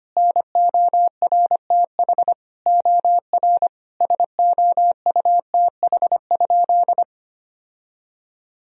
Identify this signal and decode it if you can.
Morse code, heard as a keyed tone, about 25 wpm, 700 Hz. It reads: NORTH OR SOUTH?
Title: Sentences from Top 1000 Words - Review Part 2 - 25wpm